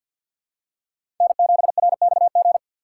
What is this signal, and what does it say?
Morse code, heard as a keyed tone, about 53 wpm, 700 Hz. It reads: N7RXG
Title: USA Call Signs - 3x Character Spacing - 50wpm